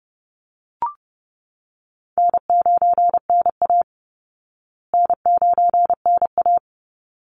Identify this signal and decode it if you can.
Morse code, heard as a keyed tone, about 30 wpm, 700 Hz. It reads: N9NA N9NA
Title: Easy Call Signs Repeated 2x - 30wpm